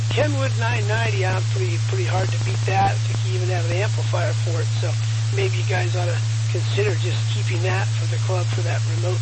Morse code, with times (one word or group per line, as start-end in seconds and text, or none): none